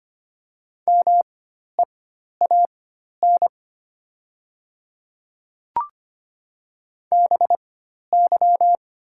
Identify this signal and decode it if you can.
Morse code, heard as a keyed tone, about 25 wpm, spaced as for 12 wpm, 700 Hz. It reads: MEAN BY
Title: 4x Character Spacing - Top 200 Words - 25wpm